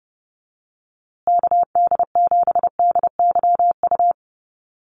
Morse code, heard as a keyed tone, about 30 wpm, 700 Hz. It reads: KD7DYU
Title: USA Call Signs Repeated 3x 30wpm